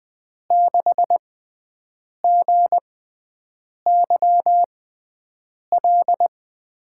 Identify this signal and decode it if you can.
Morse code, heard as a keyed tone, about 20 wpm, spaced as for 7 wpm, 700 Hz. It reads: 6GYL
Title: USA Call Signs - 6 Characters - 6x Character Spacing - 20wpm